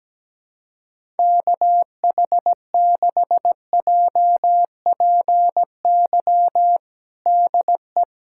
Morse code, heard as a keyed tone, about 17 wpm, 700 Hz. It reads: KH6JPY DE
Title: USA Callsign DE Callsign 17wpm